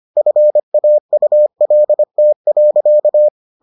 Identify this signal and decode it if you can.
Morse code, heard as a keyed tone, about 25 wpm, 600 Hz. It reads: FAULT.